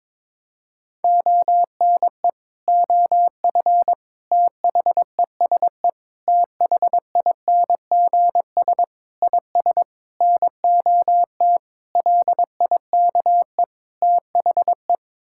Morse code, heard as a keyed tone, about 22 wpm, 700 Hz. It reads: ONE OF THESE THINGS IS NOT LIKE THE
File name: practice / 22wpm / Sets of 8 Words - 22wpm